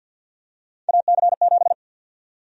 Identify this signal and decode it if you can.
Morse code, heard as a keyed tone, about 48 wpm, 700 Hz. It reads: AC7
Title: USA Call Sign Prefixes - Repeated 2x - 50wpm